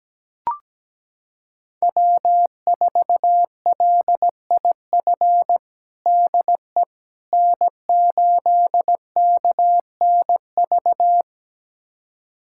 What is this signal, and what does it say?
Morse code, heard as a keyed tone, about 17 wpm, 700 Hz. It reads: W4LIF DE N8KNV
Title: USA Callsign DE Callsign - 5 Characters - 17wpm